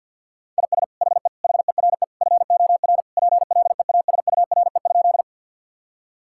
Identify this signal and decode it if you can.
Morse code, heard as a keyed tone, about 51 wpm, 700 Hz. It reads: IS HE HERE FOR PLEASURE?
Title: Sentences from Top 1600 Words 50wpm